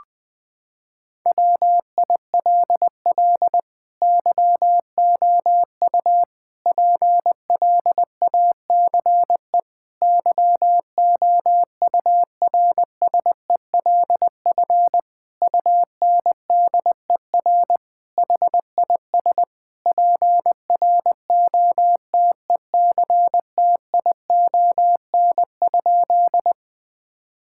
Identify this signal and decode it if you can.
Morse code, heard as a keyed tone, about 20 wpm, 700 Hz. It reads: WILL YOU PLACE YOURSELF UNDER HIS PROTECTION?
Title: Sentences from Top 1600 Words 20wpm